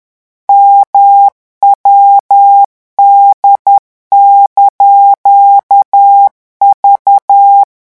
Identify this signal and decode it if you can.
Morse code, heard as a keyed tone, about 10 wpm, 800 Hz. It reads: MWD)V
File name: morse mp3 / revision19-vitesse 10 mots